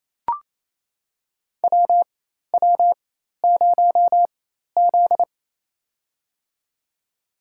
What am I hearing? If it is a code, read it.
Morse code, heard as a keyed tone, about 28 wpm, spaced as for 13 wpm, 700 Hz. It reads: WW0Z
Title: USA Call Signs - 4x Character Spacing - 28wpm